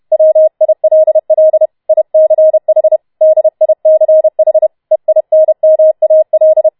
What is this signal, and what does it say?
Morse code, heard as a keyed tone, about 31 wpm, 630 Hz. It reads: WILL ICH DICH EINMAL
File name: Schneewittchen Teil4 32wpm